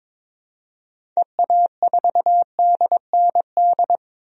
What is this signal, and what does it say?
Morse code, heard as a keyed tone, about 22 wpm, 700 Hz. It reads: EA4DND
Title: DXCC-100 Call Signs - 6 Characters - 22wpm